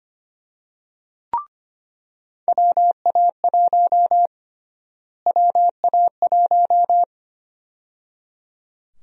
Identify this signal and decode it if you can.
Morse code, heard as a keyed tone, about 25 wpm, 700 Hz. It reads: WA1 WA1
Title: USA Call Sign Prefixes - Repeated 2x - 25wpm